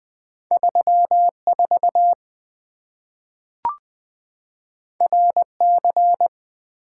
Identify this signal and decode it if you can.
Morse code, heard as a keyed tone, about 20 wpm, 700 Hz. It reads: 34 RC
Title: Two Letter-Number Combinations - 20wpm